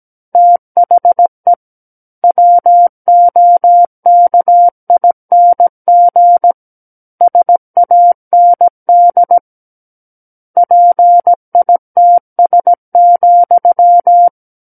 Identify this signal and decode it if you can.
Morse code, heard as a keyed tone, about 17 wpm, 700 Hz. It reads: THE WOKING SAND PITS,
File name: Book One / 08-Chapter 8 - 17 WPM